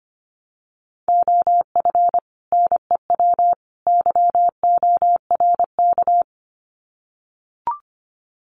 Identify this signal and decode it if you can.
Morse code, heard as a keyed tone, about 25 wpm, 700 Hz. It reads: OF NEW YORK
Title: Sets of 3 Words 25wpm